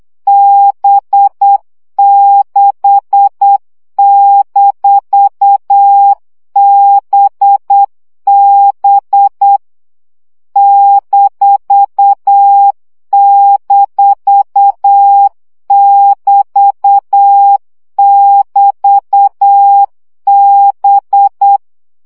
Morse code, heard as a keyed tone, about 8 wpm, 800 Hz. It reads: B6-BB --<BT><BT>B
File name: CW gyakorlas b 6 egyenlo minusz 7 baud